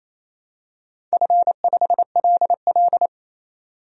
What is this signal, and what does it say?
Morse code, heard as a keyed tone, about 29 wpm, 690 Hz. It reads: F5LL
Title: DXCC-100 Call Signs - 4 Characters - 28wpm